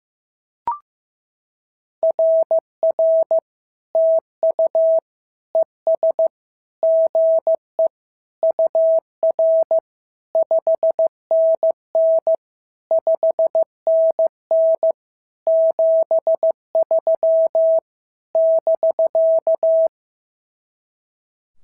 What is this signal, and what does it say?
Morse code, heard as a keyed tone, about 14 wpm, 650 Hz. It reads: RR TU ES GE UR 5NN 5NN 73 <BK>
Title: SOTA QSO Practice 15wpm